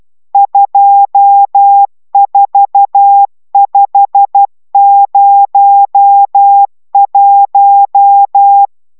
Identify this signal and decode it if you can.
Morse code, heard as a keyed tone, about 12 wpm, 800 Hz. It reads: 24501